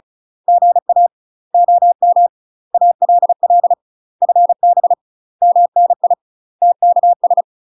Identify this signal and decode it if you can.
Morse code, heard as a keyed tone, about 36 wpm, 700 Hz. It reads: GA OM ALL FB MNI TKS
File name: UK MASTER QSOs TRAINING / 35WPM / UK MASTER QSOs 0036